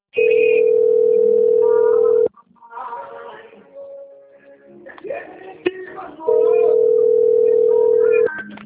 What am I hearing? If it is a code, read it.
Ringback tone (North America), 440 + 480 Hz.